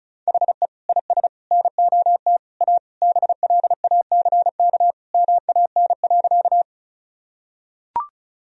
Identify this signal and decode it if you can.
Morse code, heard as a keyed tone, about 34 wpm, 700 Hz. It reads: HE IS NOT A BLACK MAN.
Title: Sentences from Top 1000 Words - Review Part 2 - 35wpm